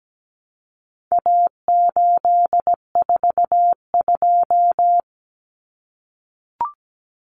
Morse code, heard as a keyed tone, about 17 wpm, 700 Hz. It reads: A842
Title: Common Cut Numbers - 4-Digit 17wpm